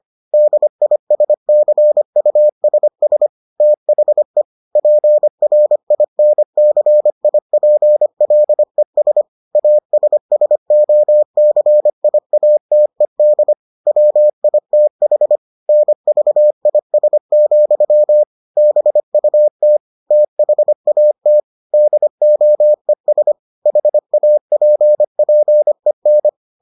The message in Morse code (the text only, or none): DISCUSS THE PRINCIPLES ASSOCIATED WITH NVIS, BUT THAT DOES HAPPEN